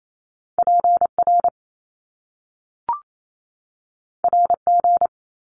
Morse code, heard as a keyed tone, about 28 wpm, 700 Hz. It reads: PR RG
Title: Top Two Letter Combinations in Words 28wpm